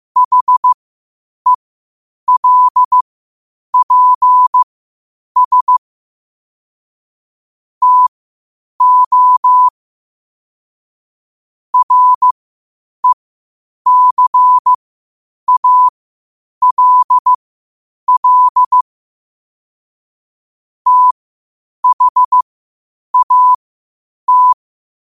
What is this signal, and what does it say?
Morse code, heard as a keyed tone, about 15 wpm, spaced as for 8 wpm, 1000 Hz. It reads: HELPS TO RECALL THAT
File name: qotdfiles / 8wpm